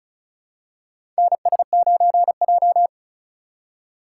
Morse code, heard as a keyed tone, about 36 wpm, 700 Hz. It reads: NS9J
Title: USA Call Signs - 4 Characters - 3x Character Spacing - 35wpm